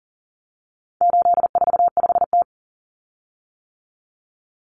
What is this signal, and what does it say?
Morse code, heard as a keyed tone, about 40 wpm, 700 Hz. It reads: 845T
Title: Common Cut Numbers - 4-Digit 40wpm